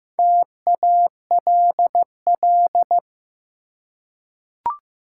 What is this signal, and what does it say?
Morse code, heard as a keyed tone, about 15 wpm, 700 Hz. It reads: TALL
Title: Four Letter Words - 15wpm